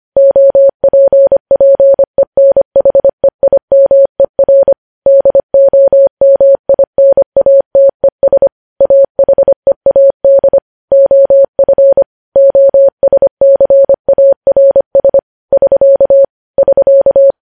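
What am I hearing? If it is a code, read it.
Morse code, heard as a keyed tone, about 25 wpm, 550 Hz. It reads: OPPENHEIMER DOMINATES AHEAD OF OSCARS <SK> <SK>
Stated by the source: Morse code news headlines sent at 25 WPM.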